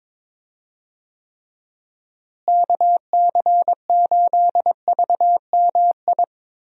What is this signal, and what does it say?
Morse code, heard as a keyed tone, about 22 wpm, 700 Hz. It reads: KC8VMI
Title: USA Call Signs - 6 Characters - Repeated 3x - 22wpm